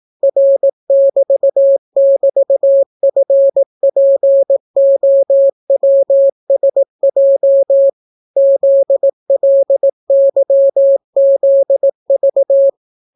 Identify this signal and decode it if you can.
Morse code, heard as a keyed tone, about 18 wpm, 550 Hz. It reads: R<BT><BT>FPOWSJ ZLYZV